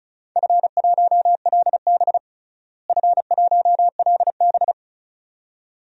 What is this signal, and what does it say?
Morse code, heard as a keyed tone, about 36 wpm, 700 Hz. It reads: F1LB F1LB
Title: DXCC-100 Call Signs - 4 Characters - Repeated 2x - 35wpm